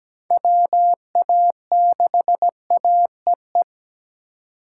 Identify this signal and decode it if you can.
Morse code, heard as a keyed tone, about 17 wpm, 700 Hz. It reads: WA6AEE
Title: K1USN SST QSO Practice - 17wpm